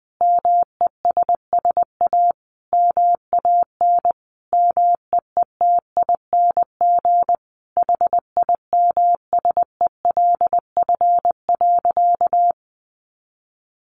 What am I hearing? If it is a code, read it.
Morse code, heard as a keyed tone, about 20 wpm, 700 Hz. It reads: MESS A MAN MEETING HIMSELF.
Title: A Sound Of Thunder 20wpm